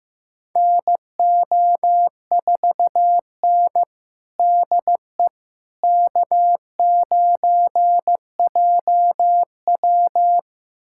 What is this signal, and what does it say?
Morse code, heard as a keyed tone, about 16 wpm, 700 Hz. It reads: NO4N DE K9JW
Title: USA Callsign DE Callsign - 4 Characters - 15wpm